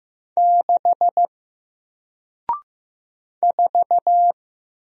Morse code, heard as a keyed tone, about 15 wpm, 700 Hz. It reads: 6 4
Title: Single Character (Letter:Number:Punctuation:Prosign) Alt 15wpm